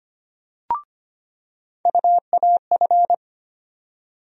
Morse code, heard as a keyed tone, about 25 wpm, 700 Hz. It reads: UAF
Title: USA Call Sign Postfixes - 3 Letters 25wpm